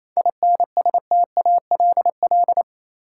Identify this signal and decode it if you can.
Morse code, heard as a keyed tone, about 28 wpm, 700 Hz. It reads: INSTALL
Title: Sentences from Top 1600 Words - New Word Drill Alt - 28wpm